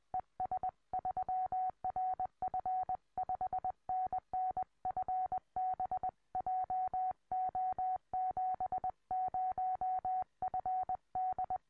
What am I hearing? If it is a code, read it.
Morse code, heard as a keyed tone, about 20 wpm, 740 Hz. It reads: ES3RF 5NN FB JO70FD
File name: tropo / es3rf